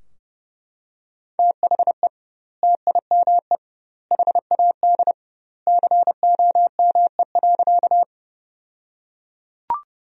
Morse code, heard as a keyed tone, about 30 wpm, 700 Hz. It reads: THE TIME HAD COME.
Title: 2x Word-Spacing - Sentences from Top 100 Words - 30wpm